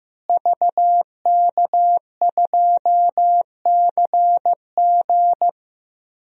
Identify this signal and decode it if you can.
Morse code, heard as a keyed tone, about 15 wpm, 700 Hz. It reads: VK2CG
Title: DXCC-100 Call Signs - 5 Characters - Repeated 2x - 15wpm